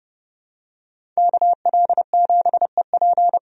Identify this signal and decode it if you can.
Morse code, heard as a keyed tone, about 30 wpm, 700 Hz. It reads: KL7EP
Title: DXCC-100 Call Signs - 5 Characters - 30wpm